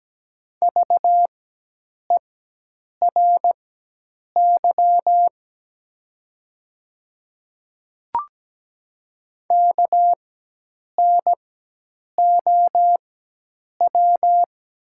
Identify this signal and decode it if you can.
Morse code, heard as a keyed tone, about 17 wpm, spaced as for 8 wpm, 700 Hz. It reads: VERY KNOW
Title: 4x Character Spacing - Top 300 Words - 17wpm